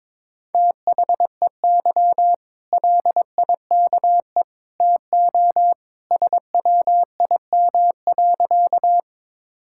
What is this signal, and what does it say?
Morse code, heard as a keyed tone, about 22 wpm, 700 Hz. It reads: THEY LIKE TO SWIM.
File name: practice / 22wpm / Easy Sentences with Hints - 22wpm